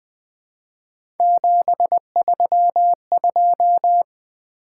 Morse code, heard as a keyed tone, about 20 wpm, 700 Hz. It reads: 732
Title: Three Letter-Number Combinations - 20wpm